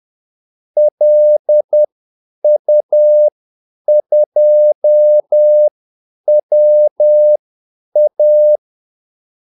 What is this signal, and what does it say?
Morse code, heard as a keyed tone, about 10 wpm, spaced as for 8 wpm, 600 Hz. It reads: LU2WA
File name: Callsigns-10-8wpm